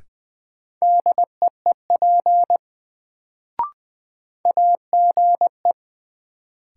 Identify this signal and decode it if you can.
Morse code, heard as a keyed tone, about 20 wpm, 700 Hz. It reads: DEEP AGE
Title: Sentences from Top 500 Words - New Word Drill - 20wpm